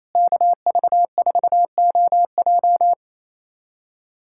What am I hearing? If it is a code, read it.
Morse code, heard as a keyed tone, about 28 wpm, 700 Hz. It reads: KV4OJ
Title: USA Callsign DE Callsign 28wpm